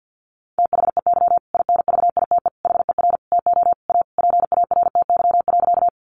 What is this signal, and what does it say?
Morse code, heard as a keyed tone, about 50 wpm, 700 Hz. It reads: THEY INVITE HER TO A PARTY.
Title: Easy Phrases 50wpm